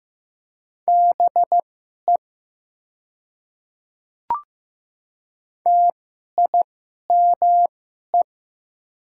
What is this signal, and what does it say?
Morse code, heard as a keyed tone, about 15 wpm, spaced as for 11 wpm, 700 Hz. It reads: BE TIME
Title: 2x Character Spacing - Top 100 Words - 15wpm